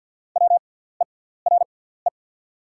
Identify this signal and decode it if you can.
Morse code, heard as a keyed tone, about 48 wpm, spaced as for 17 wpm, 690 Hz. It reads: WERE
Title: 6x Character Spacing - Top 200 Words - 50wpm